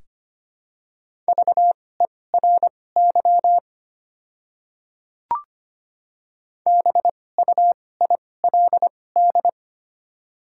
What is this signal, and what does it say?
Morse code, heard as a keyed tone, about 25 wpm, spaced as for 18 wpm, 700 Hz. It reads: VERY BUILD
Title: 2x Character Spacing - Top 200 Words - 25wpm